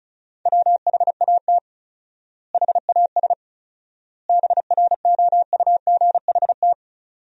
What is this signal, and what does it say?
Morse code, heard as a keyed tone, about 34 wpm, 700 Hz. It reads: WHAT HAS BROUGHT
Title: 4x Word-Spacing - Sentences from Top 500 Words - 35wpm